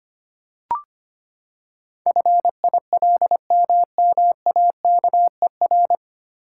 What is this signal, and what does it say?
Morse code, heard as a keyed tone, about 25 wpm, 700 Hz. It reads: FILMMAKER
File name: Nine Letter Words - 25wpm